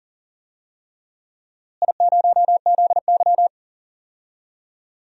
Morse code, heard as a keyed tone, about 39 wpm, 700 Hz. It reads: I0ZY
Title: DXCC-100 Call Signs - 4 Characters - Repeated 3x - 40wpm